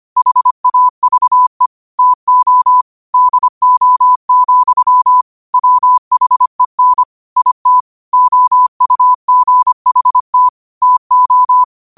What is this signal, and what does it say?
Morse code, heard as a keyed tone, about 25 wpm, 1000 Hz. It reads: HAVE TO DO, WHEN IT OUGHT TO
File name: qotdfiles / 25wpm